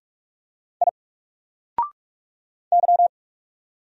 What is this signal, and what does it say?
Morse code, heard as a keyed tone, about 44 wpm, 700 Hz. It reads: I Y
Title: Single Character (Letter:Number:Punctuation:Prosign) 45wpm